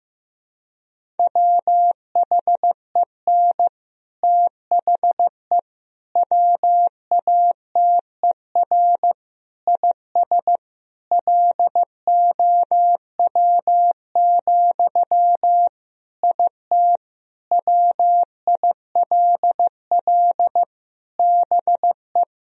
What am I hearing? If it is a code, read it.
Morse code, heard as a keyed tone, about 15 wpm, 700 Hz. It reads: WHEN THE WATER IS LOW, IT WILL BE
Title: Sentences from Top 1000 Words - Review Part 1 - 15wpm